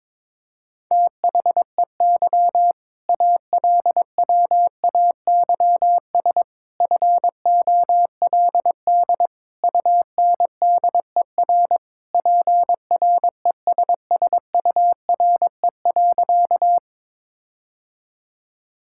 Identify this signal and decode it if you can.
Morse code, heard as a keyed tone, about 21 wpm, 700 Hz. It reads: THEY ALWAYS FOLD UNDER PRESSURE.
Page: morse-code-ninja-practice-22wpm